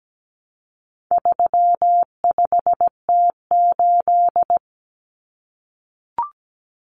Morse code, heard as a keyed tone, about 17 wpm, 700 Hz. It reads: 35T8
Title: Common Cut Numbers - 4-Digit 17wpm